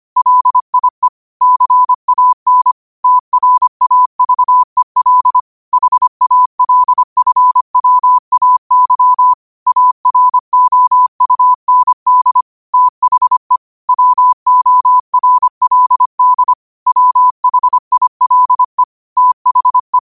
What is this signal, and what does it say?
Morse code, heard as a keyed tone, about 25 wpm, 1000 Hz. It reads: LIE CAN TRAVEL HALFWAY AROUND THE WORLD WHILE THE
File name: qotdfiles / 25wpm